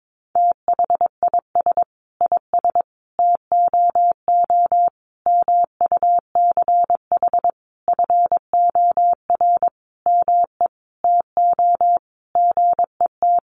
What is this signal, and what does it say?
Morse code, heard as a keyed tone, about 22 wpm, 700 Hz. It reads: THIS IS TOO MUCH FOR ME TO GET
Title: A Sound Of Thunder 22wpm